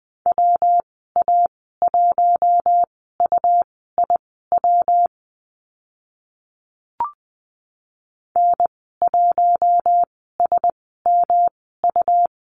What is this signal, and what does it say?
Morse code, heard as a keyed tone, about 20 wpm, spaced as for 14 wpm, 700 Hz. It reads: WA1UIW N1SMU
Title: 167 Lesson 27g - Number 7 - Call Signs 20wpm